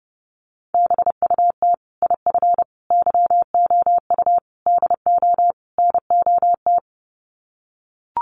Morse code, heard as a keyed tone, about 30 wpm, 700 Hz. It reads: BUT IF YOU DO NOT
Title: Sets of 5 Words Encore 30wpm